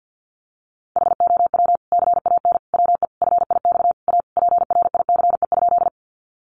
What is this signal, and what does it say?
Morse code, heard as a keyed tone, about 50 wpm, 700 Hz. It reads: HOW CAN WE FIX A PRICE?
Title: Sentences from Top 1400 Words 50wpm